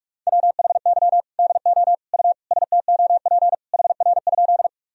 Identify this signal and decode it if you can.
Morse code, heard as a keyed tone, about 46 wpm, 700 Hz. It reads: WHY DO U STOP HR?
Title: Sentences from Top 300 Words and Abbreviations 45wpm